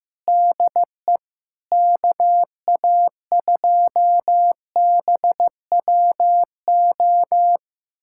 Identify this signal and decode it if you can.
Morse code, heard as a keyed tone, about 15 wpm, 700 Hz. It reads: DE KA2BWO
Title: USA Callsign DE Callsign - 6 Characters - 15wpm